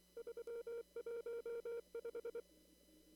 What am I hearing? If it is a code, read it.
Morse code, heard as a keyed tone, about 24 wpm, 470 Hz. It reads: 315
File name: CW / kt5j wpx cw-op-k5pi